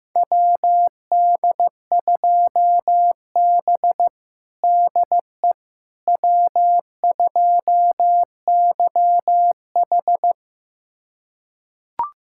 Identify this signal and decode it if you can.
Morse code, heard as a keyed tone, about 15 wpm, 700 Hz. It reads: WD2B DE W2YH
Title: USA Callsign DE Callsign - 4 Characters - 15wpm